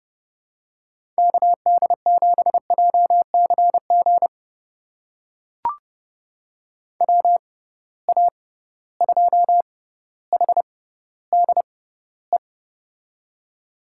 Morse code, heard as a keyed tone, about 30 wpm, spaced as for 15 wpm, 700 Hz. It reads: KD7JCG WA2HDE
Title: USA Call Signs - 6 Characters - 6x Character Spacing - 30wpm